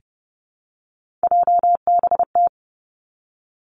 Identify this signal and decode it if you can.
Morse code, heard as a keyed tone, about 30 wpm, 700 Hz. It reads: JBT
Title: Three Challenging Letter-Number Combinations 30wpm